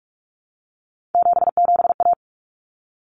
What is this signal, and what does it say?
Morse code, heard as a keyed tone, about 44 wpm, 700 Hz. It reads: 77A
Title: Common Cut Numbers - 3-Digit 45wpm